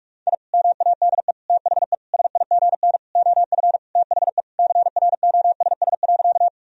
Morse code, heard as a keyed tone, about 46 wpm, 700 Hz. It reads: I MADE THE SIGN OF THE CROSS.
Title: Sentences from Top 1000 Words - Review Part 1 - 45wpm